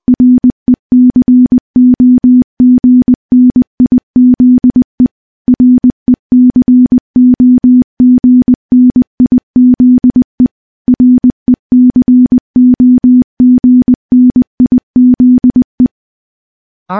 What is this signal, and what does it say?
Morse code, heard as a keyed tone, about 20 wpm, 250 Hz. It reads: RECOGNIZE RECOGNIZE RECOGNIZE
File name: morse / morse-training-20WPM